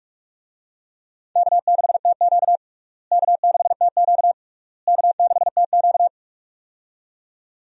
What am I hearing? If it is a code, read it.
Morse code, heard as a keyed tone, about 46 wpm, 700 Hz. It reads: K6TQ K6TQ K6TQ
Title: USA Call Signs - 4 Characters - Repeated 3x - 45wpm